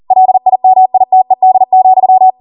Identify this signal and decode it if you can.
Morse code, heard as a keyed tone, about 40 wpm, 750 Hz. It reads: LIMITED,